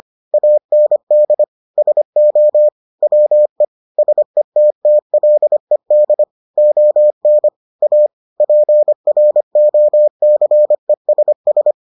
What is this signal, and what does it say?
Morse code, heard as a keyed tone, about 25 wpm, 600 Hz. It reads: AND SO WE SETTLED ON A PROCESS